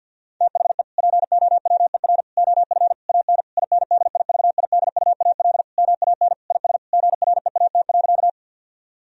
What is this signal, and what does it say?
Morse code, heard as a keyed tone, about 50 wpm, 700 Hz. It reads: THE POWER OF AN INDIVIDUAL MAN IS GREAT.
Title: Sentences from Top 1000 Words - Review Part 4 - 50wpm